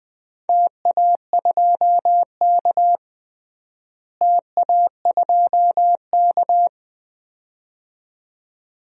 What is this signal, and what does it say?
Morse code, heard as a keyed tone, about 21 wpm, 700 Hz. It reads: TA2K TA2K
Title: DXCC-100 Call Signs - 4 Characters - Repeated 2x - 20wpm